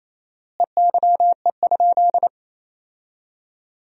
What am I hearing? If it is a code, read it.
Morse code, heard as a keyed tone, about 28 wpm, 700 Hz. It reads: EYE?
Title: 4x Word-Spacing - Sentences from Top 500 Words - 28wpm